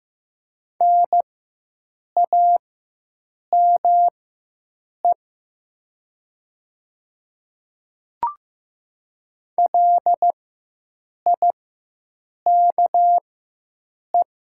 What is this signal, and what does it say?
Morse code, heard as a keyed tone, about 15 wpm, spaced as for 7 wpm, 700 Hz. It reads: NAME LIKE
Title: 4x Character Spacing - Top 300 Words - 15wpm